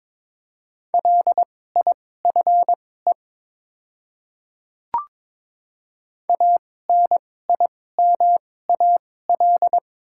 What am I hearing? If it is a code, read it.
Morse code, heard as a keyed tone, about 22 wpm, spaced as for 16 wpm, 700 Hz. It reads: LIFE ANIMAL
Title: 2x Character Spacing - Top 500 Words - 22wpm